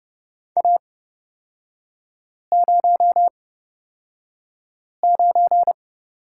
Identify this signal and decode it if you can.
Morse code, heard as a keyed tone, about 30 wpm, spaced as for 5 wpm, 700 Hz. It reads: A09
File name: Lesson 36e - BT - Three Characters 30wpm